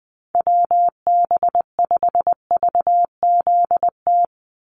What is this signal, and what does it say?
Morse code, heard as a keyed tone, about 20 wpm, 700 Hz. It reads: WB5VZT
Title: USA Call Signs 20wpm